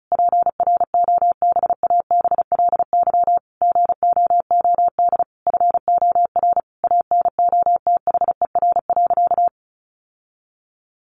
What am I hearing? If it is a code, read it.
Morse code, heard as a keyed tone, about 34 wpm, 700 Hz. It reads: PROBABLY GOOD FOR ANOTHER.
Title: Sentences from Top 1400 Words 35wpm